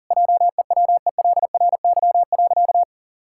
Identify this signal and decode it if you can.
Morse code, heard as a keyed tone, about 40 wpm, 700 Hz. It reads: JEWELRY.
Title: Pangram Challenge v2 - 40wpm